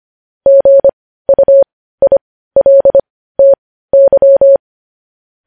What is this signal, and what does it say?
Morse code, heard as a keyed tone, about 25 wpm, spaced as for 15 wpm, 550 Hz. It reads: GUILTY
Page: News headlines in Morse code at 15 wpm